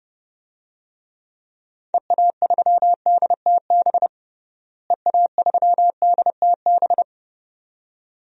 Morse code, heard as a keyed tone, about 30 wpm, 700 Hz. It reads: EA3DTB EA3DTB
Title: DXCC-100 Call Signs - 6 Characters - Repeated 2x - 30wpm